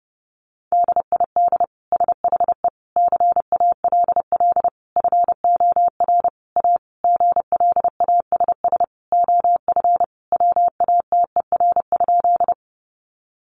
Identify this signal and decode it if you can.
Morse code, heard as a keyed tone, about 30 wpm, 700 Hz. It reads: DID SHE CALL FOR A GLASS OF WATER?
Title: Sentences from Top 1300 Words 30wpm